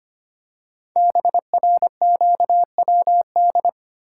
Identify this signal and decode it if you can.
Morse code, heard as a keyed tone, about 25 wpm, 700 Hz. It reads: BRQWD
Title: Code Groups - 5 Letters - 25wpm